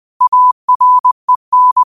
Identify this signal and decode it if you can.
Morse code, heard as a keyed tone, about 20 wpm, 1000 Hz. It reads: AREN